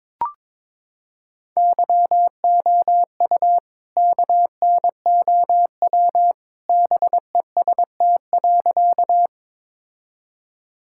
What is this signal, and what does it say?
Morse code, heard as a keyed tone, about 22 wpm, 700 Hz. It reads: YOU KNOW BEST.
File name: practice / 22wpm / Sentences with 2-Word Phrases - 22wpm